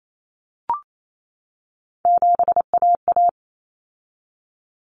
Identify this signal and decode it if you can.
Morse code, heard as a keyed tone, about 28 wpm, 700 Hz. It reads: 7AA
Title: Common Cut Numbers - 3-Digit 28wpm